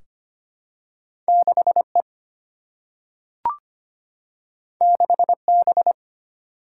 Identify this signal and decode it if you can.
Morse code, heard as a keyed tone, about 25 wpm, 700 Hz. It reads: 6 E 6 B
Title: Single-Letter Single-Number 25wpm